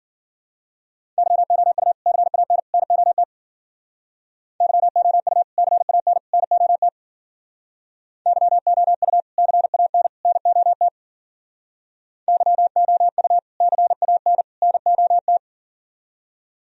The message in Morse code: YOU CAN NOT YOU CAN NOT YOU CAN NOT YOU CAN NOT

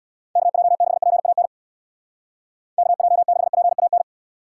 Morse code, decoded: KJ6PAN KJ6PAN

